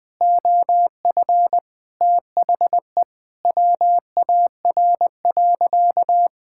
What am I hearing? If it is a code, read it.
Morse code, heard as a keyed tone, about 20 wpm, 700 Hz. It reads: OF THE WAR.
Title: Sentences with 4-Word Phrases - 20wpm